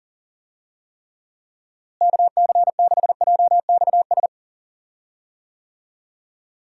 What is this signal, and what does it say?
Morse code, heard as a keyed tone, about 40 wpm, 700 Hz. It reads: KC6JXS
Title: USA Call Signs - 6 Characters - Repeated 2x - 40wpm